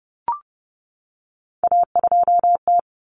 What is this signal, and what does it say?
Morse code, heard as a keyed tone, about 30 wpm, 700 Hz. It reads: A2T
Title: Common Cut Numbers - 3-Digit 30wpm